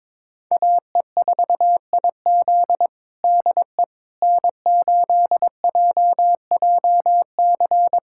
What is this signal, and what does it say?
Morse code, heard as a keyed tone, about 22 wpm, 700 Hz. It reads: AE4IZ DE N8JJC
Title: USA Callsign DE Callsign - 5 Characters - 22wpm